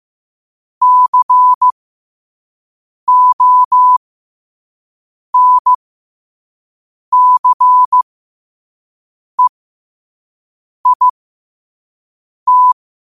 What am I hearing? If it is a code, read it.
Morse code, heard as a keyed tone, about 15 wpm, spaced as for 5 wpm, 1000 Hz. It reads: CONCEIT
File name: qotdfiles / 5wpm